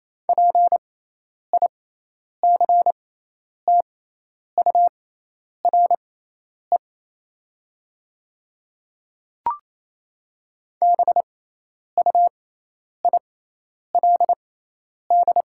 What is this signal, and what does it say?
Morse code, heard as a keyed tone, about 28 wpm, spaced as for 10 wpm, 700 Hz. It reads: PICTURE BUILD
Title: 6x Character Spacing - Top 200 Words - 28wpm